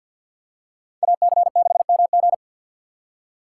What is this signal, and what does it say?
Morse code, heard as a keyed tone, about 53 wpm, 700 Hz. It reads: AC6MG
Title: USA Call Signs - 5 Characters - 6x Character Spacing - 50wpm